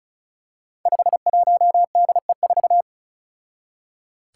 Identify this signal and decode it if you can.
Morse code, heard as a keyed tone, about 36 wpm, 700 Hz. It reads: 51DE4
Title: Five Challenging Letter-Number Combinations 35wpm